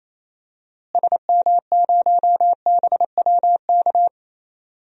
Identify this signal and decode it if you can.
Morse code, heard as a keyed tone, about 27 wpm, 700 Hz. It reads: SM0BWK
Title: DXCC-100 Call Signs - 6 Characters - Repeated 3x - 28wpm